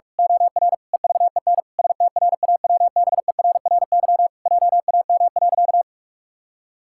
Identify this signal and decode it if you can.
Morse code, heard as a keyed tone, about 41 wpm, 700 Hz. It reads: OR EVEN STRAWBERRY JAM.
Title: Pangram Challenge v2 - 45wpm